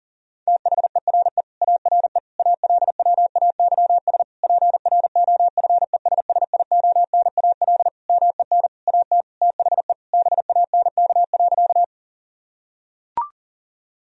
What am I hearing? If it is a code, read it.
Morse code, heard as a keyed tone, about 40 wpm, 700 Hz. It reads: THERE ARE ALWAYS PROFESSIONAL MEN AT THE BANK.
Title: Sentences from Top 1000 Words - Review Part 4 - 40wpm